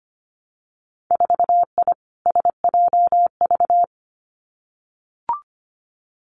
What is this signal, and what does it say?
Morse code, heard as a keyed tone, about 25 wpm, 700 Hz. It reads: 4I SJV
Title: Winter Field Day Exchanges Encore 25wpm